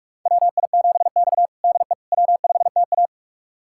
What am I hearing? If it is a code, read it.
Morse code, heard as a keyed tone, about 42 wpm, 700 Hz. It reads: WI7X DE W5TA
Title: USA Callsign DE Callsign - 4 Characters - 45wpm